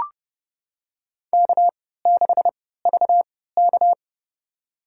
Morse code, heard as a keyed tone, about 31 wpm, spaced as for 17 wpm, 700 Hz. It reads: K6VK
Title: USA Call Signs - 3x Character Spacing - 30wpm